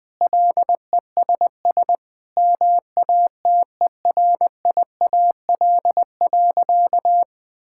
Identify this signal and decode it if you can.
Morse code, heard as a keyed tone, about 19 wpm, 700 Hz. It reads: LESS MATERIAL.
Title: Sentences from Top 1000 Words - Review Part 5 - 20wpm